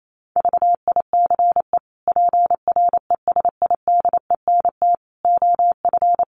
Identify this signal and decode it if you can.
Morse code, heard as a keyed tone, about 28 wpm, 700 Hz. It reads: VICE PRESIDENT OF
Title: Sets of 5 Words Encore2 28wpm